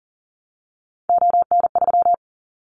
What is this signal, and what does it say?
Morse code, heard as a keyed tone, about 37 wpm, 700 Hz. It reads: ON3
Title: Callsign Prefixes for Top 25 DXCC - Repeated 3x - 40wpm